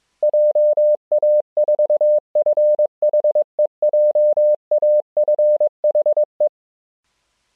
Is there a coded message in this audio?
Yes: Morse code from 0.2 to 6.5 s, heard as a keyed tone, about 22 wpm, 600 Hz. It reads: JA4FHE JAFHE